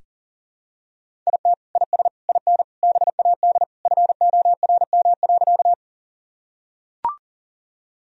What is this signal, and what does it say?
Morse code, heard as a keyed tone, about 40 wpm, 700 Hz. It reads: IT IS IN BAD FORM.
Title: Sentences from Top 1000 Words - Review Part 1 - 40wpm